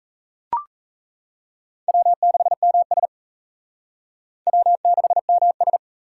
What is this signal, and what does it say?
Morse code, heard as a keyed tone, about 34 wpm, 700 Hz. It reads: W6MS W6MS
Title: Speed Racing - USA Call Signs 28wpm